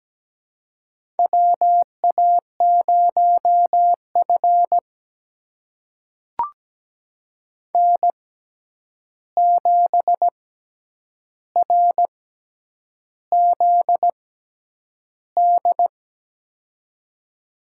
Morse code, heard as a keyed tone, about 17 wpm, spaced as for 8 wpm, 700 Hz. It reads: WA0F N7RZD
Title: USA Call Signs - 6x Character Spacing - 17wpm